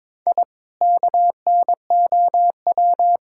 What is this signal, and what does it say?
Morse code, heard as a keyed tone, about 22 wpm, 700 Hz. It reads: I KNOW
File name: Sentences with 7-Word Phrases - 22wpm